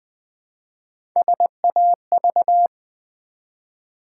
Morse code, heard as a keyed tone, about 20 wpm, 700 Hz. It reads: SAV
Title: Build A Word - 1 Letter At A Time - 3 to 5-Letter Words - 20wpm